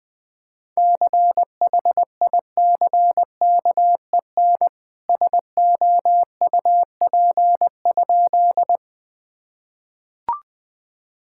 Morse code, heard as a keyed tone, about 20 wpm, 700 Hz. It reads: CHICKEN SOUP?
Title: Sentences from Top 1000 Words - Review Part 6 - 20wpm